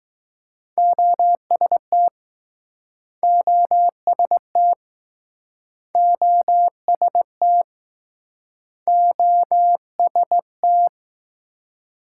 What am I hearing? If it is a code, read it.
Morse code, heard as a keyed tone, about 20 wpm, 700 Hz. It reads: OST OST OST OST